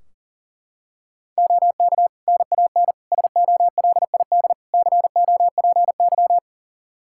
Morse code, heard as a keyed tone, about 40 wpm, 700 Hz. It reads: OK NAN SOLID COPY
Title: Practice QSO - Exchange 3 - More Info - 40wpm